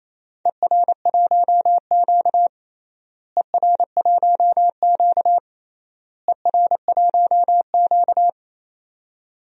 Morse code, heard as a keyed tone, about 28 wpm, 700 Hz. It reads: ER1Q ER1Q ER1Q
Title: DXCC-100 Call Signs - 4 Characters - Repeated 3x - 28wpm